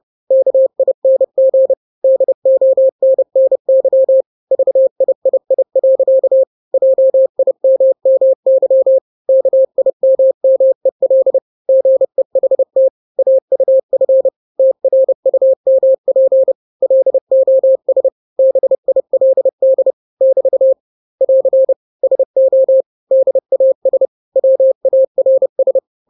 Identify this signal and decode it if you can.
Morse code, heard as a keyed tone, about 30 wpm, 530 Hz. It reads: KING DONNY VIII. JIMMY KIMMEL GEHT AUF TRUMP LOS BILD <BT> <AR> SO DAS WARS
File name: 29wpm